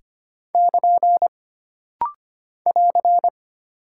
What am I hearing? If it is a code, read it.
Morse code, heard as a keyed tone, about 25 wpm, 700 Hz. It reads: <KN> <AR>
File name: Single Punctuation-Prosign - Rapid-Fire - 25wpm